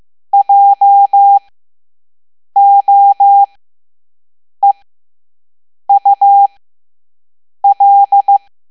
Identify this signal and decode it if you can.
Morse code, heard as a keyed tone, about 15 wpm, spaced as for 6 wpm, 800 Hz. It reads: JOEUL
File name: Morse / Level7-All-A